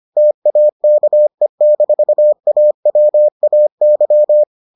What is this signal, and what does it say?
Morse code, heard as a keyed tone, about 25 wpm, 600 Hz. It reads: TAKE-AWAY